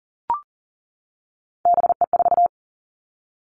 Morse code, heard as a keyed tone, about 41 wpm, 700 Hz. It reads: BE4
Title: Three Challenging Letter-Number Combinations 40wpm